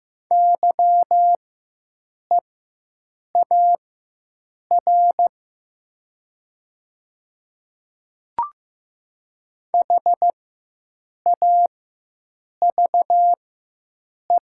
Morse code, heard as a keyed tone, about 15 wpm, spaced as for 7 wpm, 700 Hz. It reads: YEAR HAVE
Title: 4x Character Spacing - Top 200 Words - 15wpm